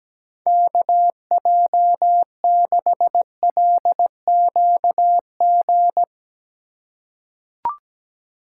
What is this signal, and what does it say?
Morse code, heard as a keyed tone, about 17 wpm, 700 Hz. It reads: KJ6LQG
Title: USA Call Signs - 6 Characters - 2x Character Spacing - 17wpm